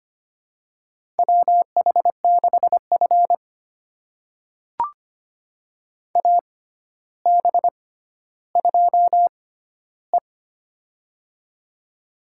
Morse code, heard as a keyed tone, about 25 wpm, spaced as for 13 wpm, 700 Hz. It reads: WH6F AB2E
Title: USA Call Signs - 4 Characters - 6x Character Spacing - 25wpm